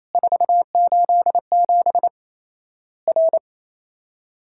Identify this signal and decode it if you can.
Morse code, heard as a keyed tone, about 28 wpm, 700 Hz. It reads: 487 R
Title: Worked All Europe DX Contest - QTC Exchange - 28wpm